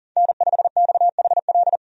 Morse code, heard as a keyed tone, about 40 wpm, 700 Hz. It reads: N5XHL
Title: USA Call Signs - 5 Characters - 40wpm